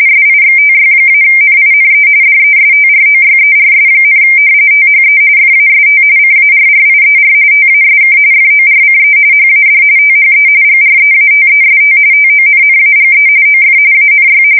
RTTY, about 45.45 baud. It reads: SIXTY-NINE DESTROYERS, THE FAST CARRIER TASK FORCE HAD THE GREATEST TONNAGE OF ANY
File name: fsk170 / battleshipnj